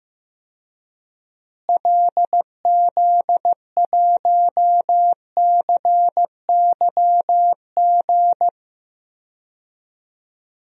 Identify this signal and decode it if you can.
Morse code, heard as a keyed tone, about 15 wpm, 700 Hz. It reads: LZ1CYG
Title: DXCC-100 Call Signs - 6 Characters - Repeated 2x - 15wpm